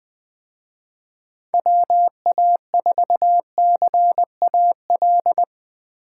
Morse code, heard as a keyed tone, about 20 wpm, 700 Hz. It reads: WA4CAL